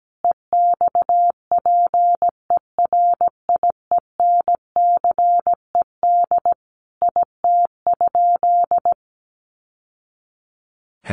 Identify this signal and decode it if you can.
Morse code, heard as a keyed tone, about 17 wpm, 700 Hz. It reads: EXPERIENCED IT?
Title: Sentences from Top 1100 Words 17wpm